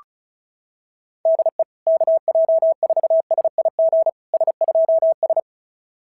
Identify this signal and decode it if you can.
Morse code, heard as a keyed tone, about 36 wpm, 650 Hz. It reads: DE KJ4SIG S2S
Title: SOTA QSO Practice 35wpm